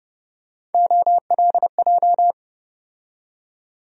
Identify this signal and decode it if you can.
Morse code, heard as a keyed tone, about 30 wpm, 700 Hz. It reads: OLJ
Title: USA Call Sign Postfixes - 3 Letters 30wpm